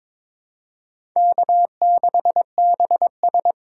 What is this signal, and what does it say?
Morse code, heard as a keyed tone, about 22 wpm, 700 Hz. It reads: K6BS
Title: USA Call Signs - 4 Characters - 3x Character Spacing - 22wpm